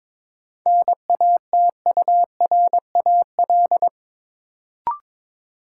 Morse code, heard as a keyed tone, about 22 wpm, 700 Hz. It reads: NATURAL
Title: Sentences from Top 700 Words - New Word Drill - 22wpm